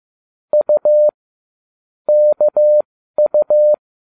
Morse code, heard as a keyed tone, about 15 wpm, spaced as for 12 wpm, 600 Hz. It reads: U KU